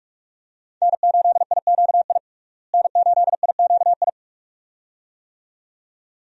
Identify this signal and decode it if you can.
Morse code, heard as a keyed tone, about 46 wpm, 700 Hz. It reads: N8IQI N8IQI
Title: USA Call Signs - 5 Characters - Repeated 2x - 45wpm